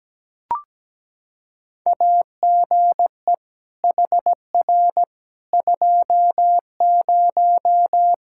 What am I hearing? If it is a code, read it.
Morse code, heard as a keyed tone, about 17 wpm, 700 Hz. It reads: AGE HR 20
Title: K1USN SST QSO Practice 17wpm